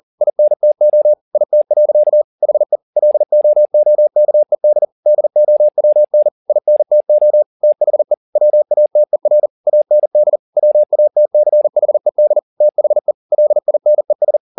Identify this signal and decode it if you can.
Morse code, heard as a keyed tone, about 40 wpm, 600 Hz. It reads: INTO IT. HE LOOKED DOWN INTO THE WATER AND WATCHED THE LINES